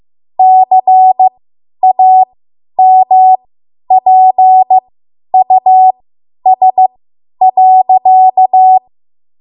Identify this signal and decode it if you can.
Morse code, heard as a keyed tone, about 15 wpm, spaced as for 10 wpm, 750 Hz. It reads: CAMPUS.
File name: Morse / 10WPM